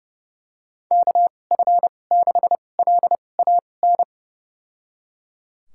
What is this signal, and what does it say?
Morse code, heard as a keyed tone, about 30 wpm, spaced as for 22 wpm, 700 Hz. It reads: KF6LAN
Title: USA Call Signs - 6 Characters - 2x Character Spacing - 30wpm